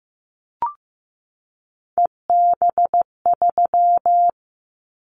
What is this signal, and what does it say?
Morse code, heard as a keyed tone, about 15 wpm, 700 Hz. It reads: EB3
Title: Callsign Prefixes for Top 25 DXCC - 15wpm